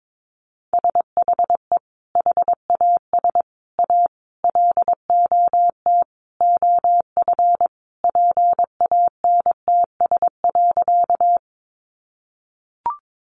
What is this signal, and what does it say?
Morse code, heard as a keyed tone, about 22 wpm, 700 Hz. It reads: SHE HAS A LOT OF PANTS.
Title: Additional Easy Phrases 22wpm